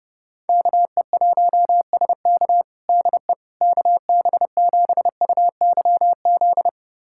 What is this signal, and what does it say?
Morse code, heard as a keyed tone, about 30 wpm, 700 Hz. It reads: KE1SK DE KB7UYZ
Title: USA Callsign DE Callsign 30wpm